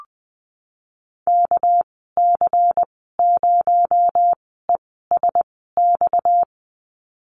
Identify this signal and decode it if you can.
Morse code, heard as a keyed tone, about 20 wpm, spaced as for 15 wpm, 700 Hz. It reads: KC0ESX
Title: 231 Lesson 37g - Letter x - Call Signs 20wpm